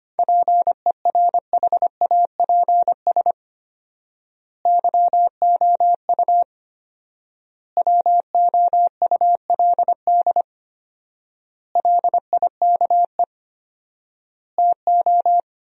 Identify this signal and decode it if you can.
Morse code, heard as a keyed tone, about 25 wpm, 700 Hz. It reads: PERHAPS YOU WOULD LIKE TO
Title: 4x Word-Spacing - Sentences from Top 500 Words - 25wpm